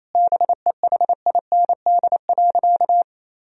Morse code, heard as a keyed tone, about 28 wpm, 700 Hz. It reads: BEHIND.
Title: Sentences from Top 1000 Words - Review Part 2 - 28wpm